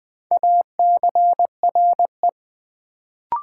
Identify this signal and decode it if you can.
Morse code, heard as a keyed tone, about 20 wpm, 700 Hz. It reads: ACRE
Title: Build A Word - 2 Letters At A Time - 4-Letter Words - 20wpm